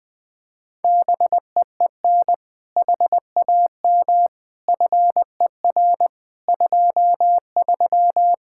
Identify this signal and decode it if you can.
Morse code, heard as a keyed tone, about 20 wpm, 700 Hz. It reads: BEEN HAM FER 23
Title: Speed-Racing Common QSO Phrases Encore - 20wpm